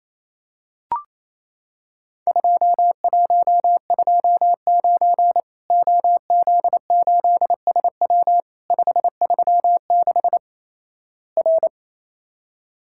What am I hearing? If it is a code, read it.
Morse code, heard as a keyed tone, about 28 wpm, 700 Hz. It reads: 2129 OZ8SW 536 R
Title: Worked All Europe DX Contest - QTC Exchange - 28wpm